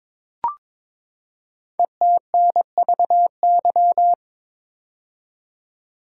Morse code, heard as a keyed tone, about 22 wpm, 700 Hz. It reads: ETNVY
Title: Code Groups - 5 Letters - 22wpm